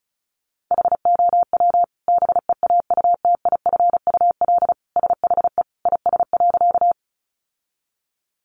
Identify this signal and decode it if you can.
Morse code, heard as a keyed tone, about 34 wpm, 700 Hz. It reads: HOW BEAUTIFUL SHE IS.
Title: Sentences from Top 900 Words 35wpm